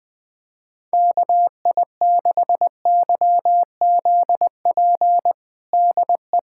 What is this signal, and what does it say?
Morse code, heard as a keyed tone, about 20 wpm, 700 Hz. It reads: KI6YZP DE
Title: USA Callsign DE Callsign 20wpm